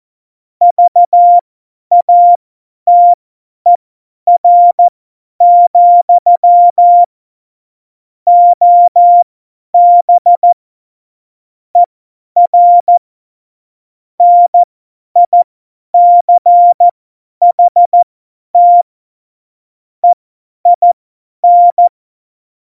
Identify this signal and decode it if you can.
Morse code, heard as a keyed tone, about 14 wpm, spaced as for 10 wpm, 700 Hz. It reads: VATER, OB ER NICHT EIN
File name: Morsekurs / Witze / Witz 10. Oktober 17, Tempo 50